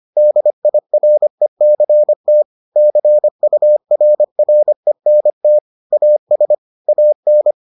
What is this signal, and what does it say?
Morse code, heard as a keyed tone, about 25 wpm, 600 Hz. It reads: DIRECT CURRENT AS AN